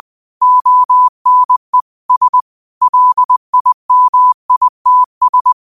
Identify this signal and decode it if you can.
Morse code, heard as a keyed tone, about 20 wpm, 1000 Hz. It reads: ONE S LIMITS
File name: qotdfiles / 20wpm